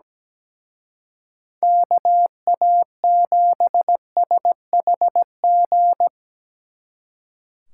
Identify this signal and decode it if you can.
Morse code, heard as a keyed tone, about 19 wpm, 700 Hz. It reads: KA7SHG
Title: Speed Racing - USA 2x3 Call Signs 17wpm